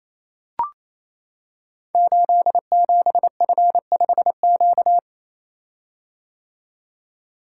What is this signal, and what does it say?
Morse code, heard as a keyed tone, about 27 wpm, 700 Hz. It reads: 87F5Q
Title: Code Groups - 5 Letter:Numbers - 28wpm